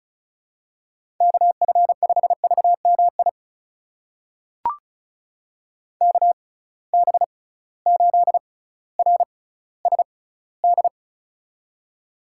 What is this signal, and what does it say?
Morse code, heard as a keyed tone, about 36 wpm, spaced as for 18 wpm, 700 Hz. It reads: KF5VMI KB8RSD
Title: USA Call Signs - 6 Characters - 6x Character Spacing - 35wpm